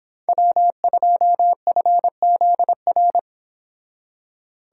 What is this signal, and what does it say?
Morse code, heard as a keyed tone, about 26 wpm, 700 Hz. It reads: W2FZR